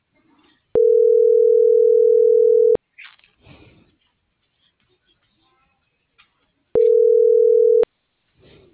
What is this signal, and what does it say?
Ringback tone (North America), 440 + 480 Hz.